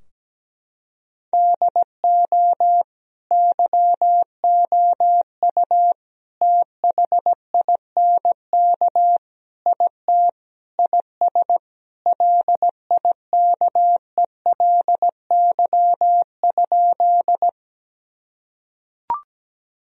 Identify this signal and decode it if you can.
Morse code, heard as a keyed tone, about 17 wpm, 700 Hz. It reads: DO YOU THINK IT IS LIKELY?
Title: Sentences from Top 1000 Words - Review Part 4 - 17wpm